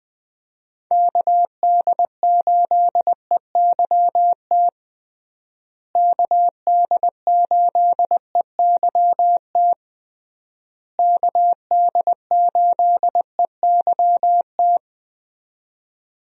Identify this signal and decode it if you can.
Morse code, heard as a keyed tone, about 20 wpm, 700 Hz. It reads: KD8EYT KD8EYT KD8EYT
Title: USA Call Signs - 6 Characters - Repeated 3x - 20wpm